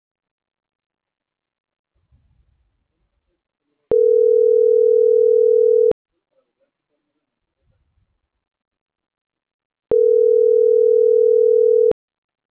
Ringback tone (North America), 440 + 480 Hz.
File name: CENTRAL MAERSK / out gabrielle.melendez